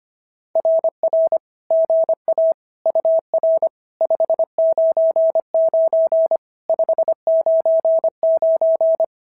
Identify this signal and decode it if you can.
Morse code, heard as a keyed tone, about 25 wpm, 650 Hz. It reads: RR GA UR 599 599
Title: SOTA QSO Practice 25wpm